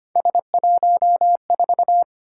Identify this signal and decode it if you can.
Morse code, heard as a keyed tone, about 25 wpm, 700 Hz. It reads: S14
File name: Three Letter-Number Combinations - 25wpm